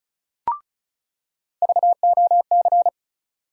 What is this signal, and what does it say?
Morse code, heard as a keyed tone, about 36 wpm, 700 Hz. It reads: VOC
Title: USA Call Sign Postfixes - 3 Letters 35wpm